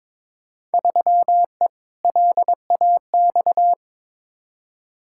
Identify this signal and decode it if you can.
Morse code, heard as a keyed tone, about 22 wpm, 700 Hz. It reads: 3E LAX
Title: Field Day Exchanges Encore - 22wpm